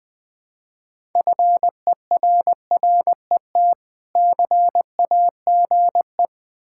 Morse code, heard as a keyed tone, about 20 wpm, 700 Hz. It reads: FERRET CAGE
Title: Difficult Words with Hints - 20wpm